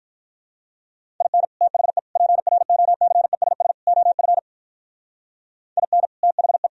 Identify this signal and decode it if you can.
Morse code, heard as a keyed tone, about 42 wpm, 700 Hz. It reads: IN THE PROCESS OF IN THE
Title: Speed-Racing - Sets of 4 Words 35wpm